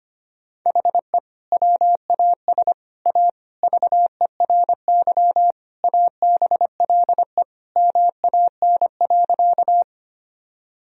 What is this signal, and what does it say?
Morse code, heard as a keyed tone, about 25 wpm, 700 Hz. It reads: HE WAS A VERY ABLE MAN.
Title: Sentences from Top 1000 Words - Review Part 2 - 25wpm